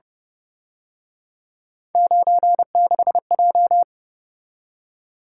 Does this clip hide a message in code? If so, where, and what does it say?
Yes: Morse code from 1.9 to 3.8 s, heard as a keyed tone, about 28 wpm, 700 Hz. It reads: 96J